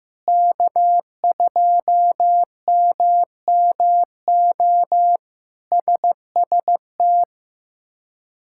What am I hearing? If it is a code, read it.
Morse code, heard as a keyed tone, about 15 wpm, 700 Hz. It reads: K2MMO SST
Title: K1USN SST QSO Practice - 15wpm